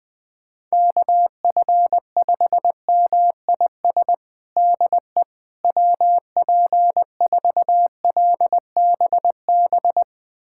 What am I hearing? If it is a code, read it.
Morse code, heard as a keyed tone, about 20 wpm, 700 Hz. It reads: KF5MIS DE WP4LBB
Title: USA Callsign DE Callsign - 6 Characters - 20wpm